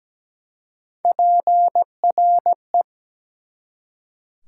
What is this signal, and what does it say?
Morse code, heard as a keyed tone, about 17 wpm, 700 Hz. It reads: PRE
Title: Prefixes and Suffixes 17wpm